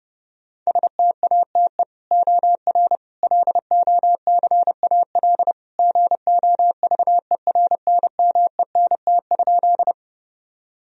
Morse code, heard as a keyed tone, about 30 wpm, 700 Hz. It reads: STATE OR LOCAL GOVERNMENT?
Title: Sentences from Top 1000 Words - Review Part 3 - 30wpm